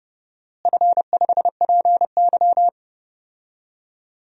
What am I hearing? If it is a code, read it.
Morse code, heard as a keyed tone, about 31 wpm, 700 Hz. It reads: F5PY
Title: DXCC-100 Call Signs - 4 Characters - Repeated 2x - 30wpm